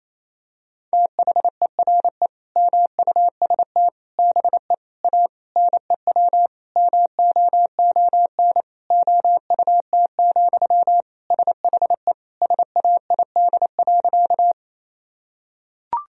Morse code, heard as a keyed tone, about 27 wpm, 700 Hz. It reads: THERE MUST BE A NEW MOON OUT, SHE SAID.
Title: Sentences from Top 1000 Words - Review Part 3 - 28wpm